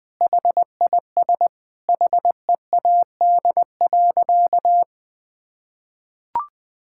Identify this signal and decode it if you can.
Morse code, heard as a keyed tone, about 21 wpm, 700 Hz. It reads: HIS HEAD.
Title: Sentences with 2-Word Phrases - Spoken Beginning - 20wpm